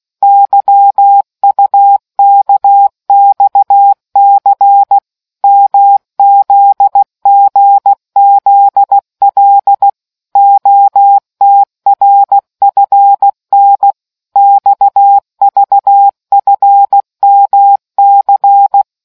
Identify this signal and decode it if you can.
Morse code, heard as a keyed tone, about 16 wpm, 800 Hz. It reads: YUKXC MZGZL OTRFN XVFMC